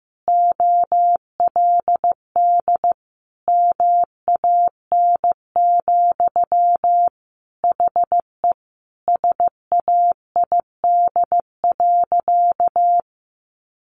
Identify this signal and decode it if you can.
Morse code, heard as a keyed tone, about 15 wpm, 700 Hz. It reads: OLD MAN, HE SAID.
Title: Sentences from Top 1500 Words 15wpm